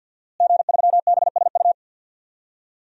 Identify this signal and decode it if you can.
Morse code, heard as a keyed tone, about 49 wpm, 700 Hz. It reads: G3BSU
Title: DXCC-100 Call Signs - 5 Characters - Repeated 3x - 50wpm